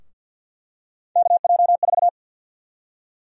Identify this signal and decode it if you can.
Morse code, heard as a keyed tone, about 48 wpm, 700 Hz. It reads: KJ4
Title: USA Call Sign Prefixes - 50wpm